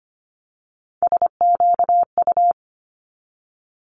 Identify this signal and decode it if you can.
Morse code, heard as a keyed tone, about 25 wpm, 700 Hz. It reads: SQU
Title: Multi-letter Phonemes 25wpm